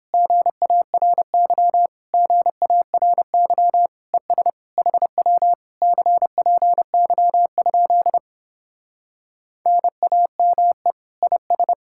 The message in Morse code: GARY GARY ES HW CPY? NAME IS

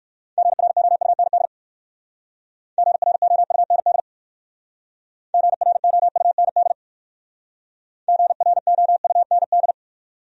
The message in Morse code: GROUND GROUND GROUND GROUND

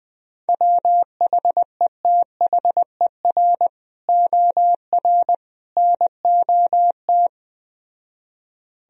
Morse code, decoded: WHETHER OR NOT

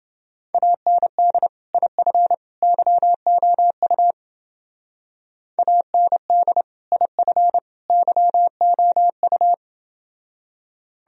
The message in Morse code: AND IF YOU AND IF YOU

